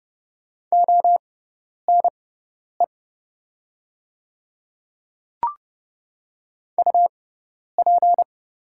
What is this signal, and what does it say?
Morse code, heard as a keyed tone, about 30 wpm, spaced as for 10 wpm, 700 Hz. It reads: ONE UP
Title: 6x Character Spacing - Top 100 Words - 30wpm